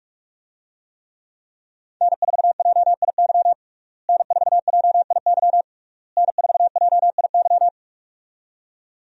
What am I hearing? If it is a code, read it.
Morse code, heard as a keyed tone, about 46 wpm, 700 Hz. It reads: N4JIY N4JIY N4JIY
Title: USA Call Signs - 5 Characters - Repeated 3x - 45wpm